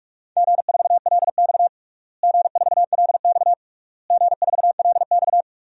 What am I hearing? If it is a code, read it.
Morse code, heard as a keyed tone, about 46 wpm, 700 Hz. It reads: G4LX G4LX G4LX
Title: DXCC-100 Call Signs - 4 Characters - Repeated 3x - 45wpm